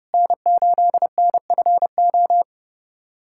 Morse code, heard as a keyed tone, about 30 wpm, 700 Hz. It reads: N8NFO
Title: USA Callsign DE Callsign - 5 Characters - 30wpm